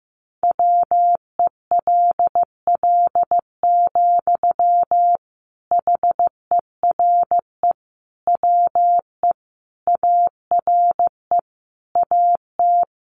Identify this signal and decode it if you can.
Morse code, heard as a keyed tone, about 15 wpm, 700 Hz. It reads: WELL, HERE WE ARE AT
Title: Sentences from Top 300 Words 15wpm